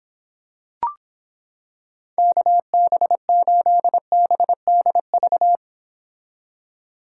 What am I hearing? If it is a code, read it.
Morse code, heard as a keyed tone, about 26 wpm, 700 Hz. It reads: KB8BDV